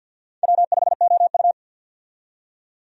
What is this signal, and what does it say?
Morse code, heard as a keyed tone, about 52 wpm, 700 Hz. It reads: W5OU
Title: USA Call Signs - 4 Characters - 50wpm